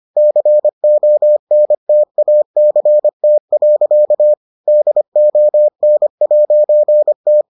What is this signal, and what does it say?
Morse code, heard as a keyed tone, about 25 wpm, 600 Hz. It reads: CONTACT. DON'T